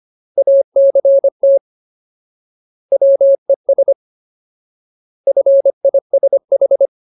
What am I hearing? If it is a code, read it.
Morse code, heard as a keyed tone, about 25 wpm, 550 Hz. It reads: ACT WES FISH
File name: Ultimate Word List / 25WPM Ultimate